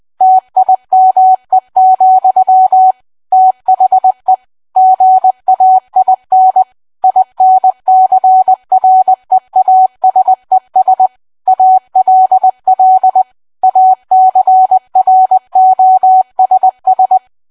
Morse code, heard as a keyed tone, about 20 wpm, 750 Hz. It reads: TIME, THE GAIN INCREASES ALL ACROSS